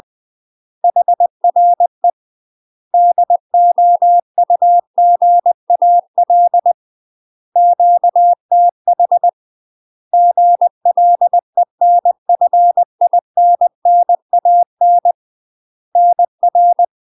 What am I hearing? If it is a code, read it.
Morse code, heard as a keyed tone, about 20 wpm, 700 Hz. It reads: HRE DOUGAL QTH GLENFINNAN NR
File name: UK MASTER QSOs TRAINING / 20WPM / UK MASTER QSOs 0026